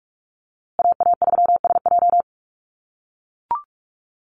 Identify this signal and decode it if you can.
Morse code, heard as a keyed tone, about 44 wpm, 700 Hz. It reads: AA3SJ
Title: USA Call Signs Repeated 3x 45wpm